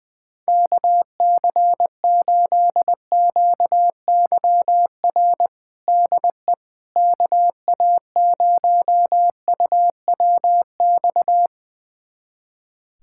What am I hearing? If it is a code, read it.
Morse code, heard as a keyed tone, about 21 wpm, 700 Hz. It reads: KC8QYR DE KA0UWX
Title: USA Callsign DE Callsign - 6 Characters - 20wpm